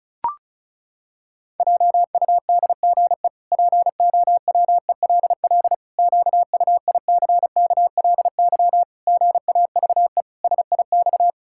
Morse code, heard as a keyed tone, about 34 wpm, 700 Hz. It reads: JUDGE POWELL QUICKLY GAVE SIX
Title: Pangram Challenge v2 - 35wpm